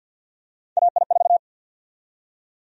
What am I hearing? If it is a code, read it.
Morse code, heard as a keyed tone, about 48 wpm, 700 Hz. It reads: AI4
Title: USA Call Sign Prefixes - 50wpm